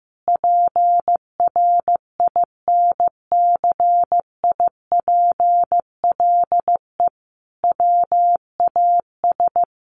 Morse code, heard as a keyed tone, about 15 wpm, 700 Hz. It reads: PRINCIPLE WAS
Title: Sentences from Top 1400 Words 15wpm